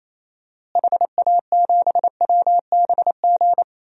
Morse code, heard as a keyed tone, about 28 wpm, 700 Hz. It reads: HA7WBG
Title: DXCC-100 Call Signs - 6 Characters - Repeated 2x - 28wpm